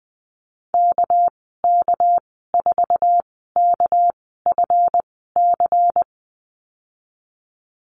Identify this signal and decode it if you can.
Morse code, heard as a keyed tone, about 20 wpm, spaced as for 14 wpm, 700 Hz. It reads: KK4KFC
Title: 207 Lesson 33g - Letter j - Call Signs 20wpm